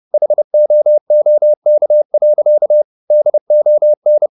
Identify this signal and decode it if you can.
Morse code, heard as a keyed tone, about 30 wpm, 600 Hz. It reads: HOOK. DON